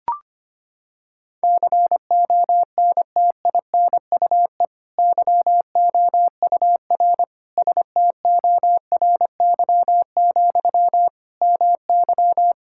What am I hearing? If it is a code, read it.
Morse code, heard as a keyed tone, about 25 wpm, 700 Hz. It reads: CONTINUE YOUR STORY, MY
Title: Sentences from Top 1000 Words - Review Part 6 - 25wpm